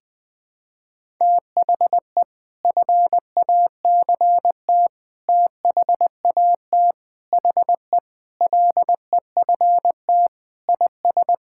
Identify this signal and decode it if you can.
Morse code, heard as a keyed tone, about 21 wpm, 700 Hz. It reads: THE FACT THAT HE LEFT IS
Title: Sentences with 2-Word Phrases - 20wpm